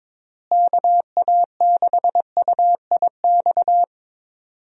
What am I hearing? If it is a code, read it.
Morse code, heard as a keyed tone, about 22 wpm, 700 Hz. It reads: KA6UIX
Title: USA Call Signs - 6 Characters - Repeated 2x - 22wpm